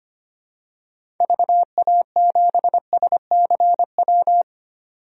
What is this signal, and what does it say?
Morse code, heard as a keyed tone, about 25 wpm, 700 Hz. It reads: VA7SCW
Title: DXCC-100 Call Signs - 6 Characters - 25wpm